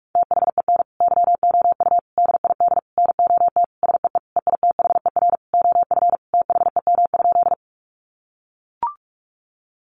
Morse code, heard as a keyed tone, about 44 wpm, 700 Hz. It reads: THEN YOU DID NOT SEE EITHER OF THEM?
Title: Sentences from Top 600 Words 45wpm